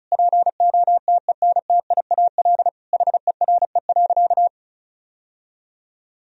Morse code, heard as a keyed tone, about 34 wpm, 700 Hz. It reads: POTENTIAL HERE.
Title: Sentences from Top 1000 Words - Review Part 4 - 35wpm